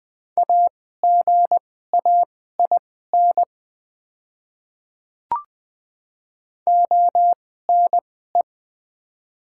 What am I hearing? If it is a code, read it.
Morse code, heard as a keyed tone, about 20 wpm, spaced as for 14 wpm, 700 Hz. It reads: AGAIN ONE
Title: 2x Character Spacing - Top 300 Words - 20wpm